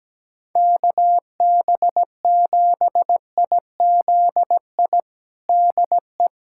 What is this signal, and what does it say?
Morse code, heard as a keyed tone, about 17 wpm, 700 Hz. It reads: KB7IZI DE
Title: USA Callsign DE Callsign - 6 Characters - 17wpm